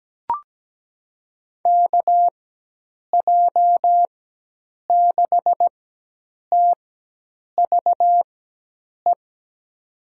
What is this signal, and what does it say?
Morse code, heard as a keyed tone, about 17 wpm, spaced as for 8 wpm, 700 Hz. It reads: KJ6TVE
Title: USA Call Signs - 6 Characters - 4x Character Spacing - 17wpm